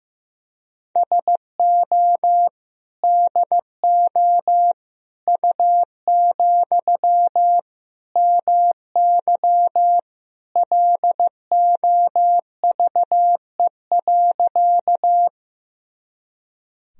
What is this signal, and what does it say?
Morse code, heard as a keyed tone, about 15 wpm, 700 Hz. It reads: SO DO U, MY LOVE.
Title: Sentences from Top 400 Words and Abbreviations 15wpm